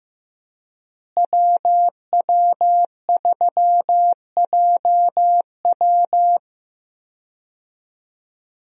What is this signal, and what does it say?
Morse code, heard as a keyed tone, about 15 wpm, 700 Hz. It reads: WW3JW
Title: USA Call Signs - 5 Characters - Repeated 3x - 15wpm